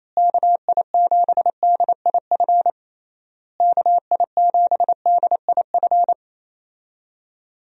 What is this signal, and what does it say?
Morse code, heard as a keyed tone, about 28 wpm, 700 Hz. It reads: KI7DIF KI7DIF
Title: USA Call Signs - 6 Characters - Repeated 3x - 28wpm